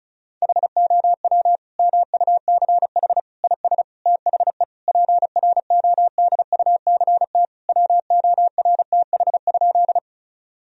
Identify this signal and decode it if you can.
Morse code, heard as a keyed tone, about 36 wpm, 700 Hz. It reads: HOW MUCH IS THE PRODUCT WORTH?
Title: Sentences from Top 1000 Words - Review Part 2 - 35wpm